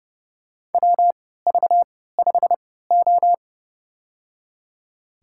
Morse code, heard as a keyed tone, about 30 wpm, spaced as for 17 wpm, 700 Hz. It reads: WV5O
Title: USA Call Signs - 3x Character Spacing - 30wpm